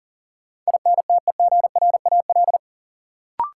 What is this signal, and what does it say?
Morse code, heard as a keyed tone, about 40 wpm, 700 Hz. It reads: INTEGRAL
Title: Sets of 5 Words - New Word Drill - 40wpm